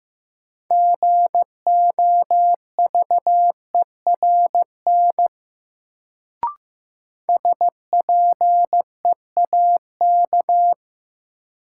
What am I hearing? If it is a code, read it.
Morse code, heard as a keyed tone, about 15 wpm, 700 Hz. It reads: GOVERN SPEAK
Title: Sentences from Top 1000 Words - Review Part 2 - New Word Drill - 15wpm